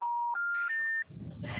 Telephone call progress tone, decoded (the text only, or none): special information tone